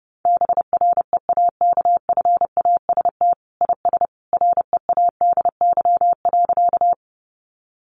Morse code, heard as a keyed tone, about 30 wpm, 700 Hz. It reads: BREAKFAST IS READY.
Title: Sentences from Top 1500 Words 30wpm